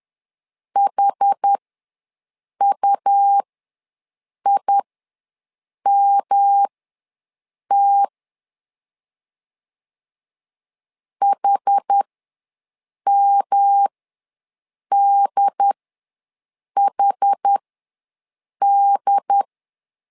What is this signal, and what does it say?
Morse code, heard as a keyed tone, about 10 wpm, spaced as for 6 wpm, 800 Hz. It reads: HUIMT HMDHD
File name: LEZIONI / 3 MWRO / 3 M ws 10wpm 2121